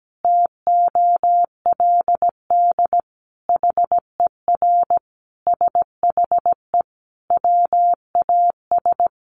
Morse code, heard as a keyed tone, about 17 wpm, 700 Hz. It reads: TOLD HER SHE WAS
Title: Sentences from Top 1200 Words 17wpm